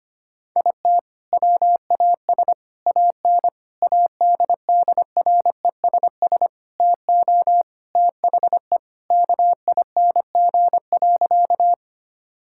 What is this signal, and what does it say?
Morse code, heard as a keyed tone, about 25 wpm, 700 Hz. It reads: IT WAS AN ADDRESS TO THE KING.
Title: Sentences from Top 1000 Words - Review Part 6 - 25wpm